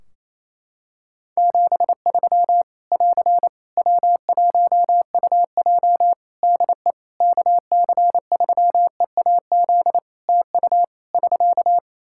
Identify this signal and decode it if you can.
Morse code, heard as a keyed tone, about 28 wpm, 700 Hz. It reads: 73 <AR> W1UJ DE KC3EAZ TU <SK>
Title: Practice QSO - Exchange 4 - Ending - 28wpm